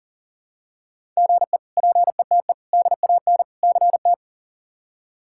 Morse code, heard as a keyed tone, about 37 wpm, 700 Hz. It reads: GE PETE DAN CT
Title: K1USN SST QSO Practice - 40wpm